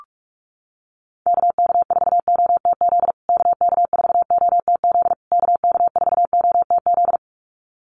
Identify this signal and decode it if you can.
Morse code, heard as a keyed tone, about 44 wpm, 700 Hz. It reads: KK4OTZ KK4OTZ KK4OTZ
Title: USA Call Signs Repeated 3x 45wpm